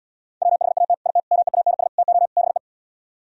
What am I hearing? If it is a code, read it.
Morse code, heard as a keyed tone, about 78 wpm, 700 Hz. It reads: WHAT IT MEANS TO BE